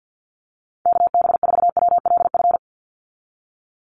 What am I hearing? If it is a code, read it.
Morse code, heard as a keyed tone, about 50 wpm, 700 Hz. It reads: KB4WLF
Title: Call Signs Repeated 3x - 50wpm